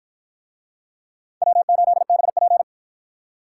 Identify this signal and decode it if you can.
Morse code, heard as a keyed tone, about 50 wpm, 700 Hz. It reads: W8BP